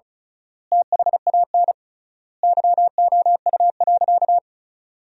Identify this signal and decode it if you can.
Morse code, heard as a keyed tone, about 36 wpm, 700 Hz. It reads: THAN YOU.
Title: 3x Word-Spacing - Sentences from Top 400 Words - 35wpm